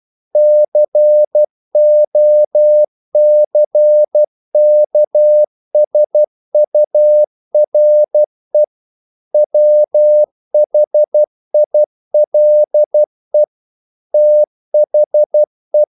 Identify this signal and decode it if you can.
Morse code, heard as a keyed tone, about 12 wpm, 600 Hz. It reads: COCKSURE WHILE THE
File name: CW / Practice / EnglishQuotations0112